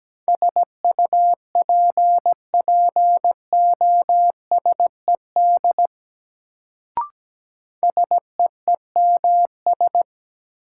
Morse code, heard as a keyed tone, about 17 wpm, 700 Hz. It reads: SUPPOSED SEEMS
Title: Sets of 3 Words - New Word Drill - 17wpm